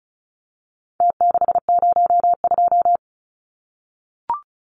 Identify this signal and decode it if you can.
Morse code, heard as a keyed tone, about 34 wpm, 700 Hz. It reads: T602
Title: Numbers - Contest Style - 4-Digit 35wpm